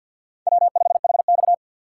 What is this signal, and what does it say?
Morse code, heard as a keyed tone, about 49 wpm, 700 Hz. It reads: W5HX
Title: USA Call Signs - 4 Characters - 2x Character Spacing - 50wpm